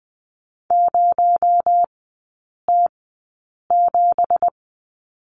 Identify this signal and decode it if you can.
Morse code, heard as a keyed tone, about 20 wpm, spaced as for 8 wpm, 700 Hz. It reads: 0T7
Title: 218 Lesson 35e - Number 0 - Three Characters 20wpm